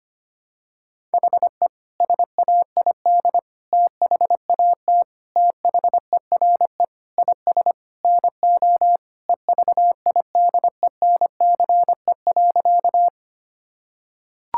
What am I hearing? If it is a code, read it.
Morse code, heard as a keyed tone, about 25 wpm, 700 Hz. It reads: HE SAID THAT THERE IS NO EVIDENCE.
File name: practice / 25wpm / Sentences with 4-Word Phrases - 25wpm